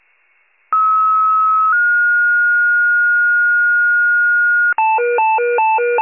Two-tone page, heard as a one-second tone, then a three-second tone, about 1290 then 1430 Hz. It is heard from 0.7 to 4.7 s.